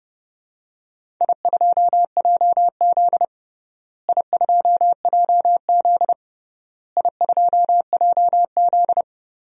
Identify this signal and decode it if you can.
Morse code, heard as a keyed tone, about 30 wpm, 700 Hz. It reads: I2JZ I2JZ I2JZ
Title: DXCC-100 Call Signs - 4 Characters - Repeated 3x - 30wpm